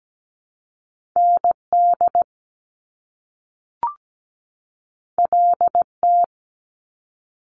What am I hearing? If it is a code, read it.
Morse code, heard as a keyed tone, about 17 wpm, 700 Hz. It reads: ND LT
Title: Top Two Letter Combinations in Words 17wpm